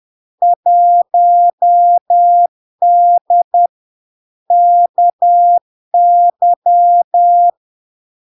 Morse code, heard as a keyed tone, about 10 wpm, 700 Hz. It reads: 1D KY
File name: LARK-FD-2022-Log-in-CW-10wpm-700-Hz